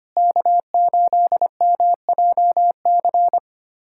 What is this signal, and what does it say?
Morse code, heard as a keyed tone, about 25 wpm, 700 Hz. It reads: K8MJC
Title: USA Callsign DE Callsign 25wpm